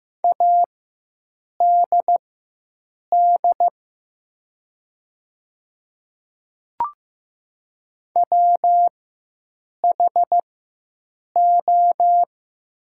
Morse code, heard as a keyed tone, about 15 wpm, spaced as for 7 wpm, 700 Hz. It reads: ADD WHO
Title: 4x Character Spacing - Top 300 Words - 15wpm